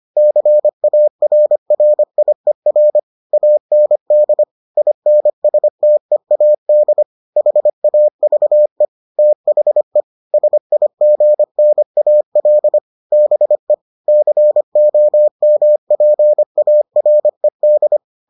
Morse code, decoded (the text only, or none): CARRIER AND INSTEAD HAVE THE SIGNAL BE COMPARED